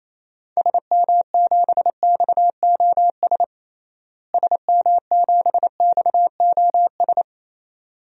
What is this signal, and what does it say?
Morse code, heard as a keyed tone, about 28 wpm, 700 Hz. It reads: SM7XOS SM7XOS
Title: DXCC-100 Call Signs - 6 Characters - Repeated 2x - 28wpm